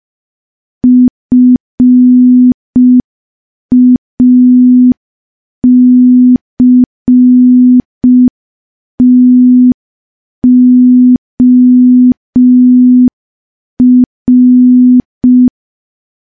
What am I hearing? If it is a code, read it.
Morse code, heard as a keyed tone, about 5 wpm, 250 Hz. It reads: FACTOR
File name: morse / morse-training-05WPM